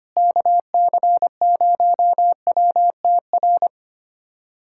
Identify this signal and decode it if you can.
Morse code, heard as a keyed tone, about 25 wpm, 700 Hz. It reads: KC0WTR
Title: USA Callsign DE Callsign 25wpm